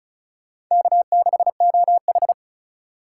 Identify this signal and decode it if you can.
Morse code, heard as a keyed tone, about 43 wpm, 700 Hz. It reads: K6OH
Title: Speed Racing - USA 1x2 Call Signs 35wpm